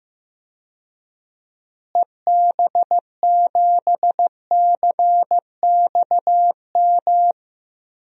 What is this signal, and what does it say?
Morse code, heard as a keyed tone, about 15 wpm, 700 Hz. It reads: EB7CXM
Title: DXCC-100 Call Signs - 6 Characters - Repeated 2x - 15wpm